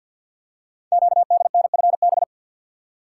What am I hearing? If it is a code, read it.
Morse code, heard as a keyed tone, about 48 wpm, 700 Hz. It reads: QDNFB
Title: Code Groups - 5 Letters - 50wpm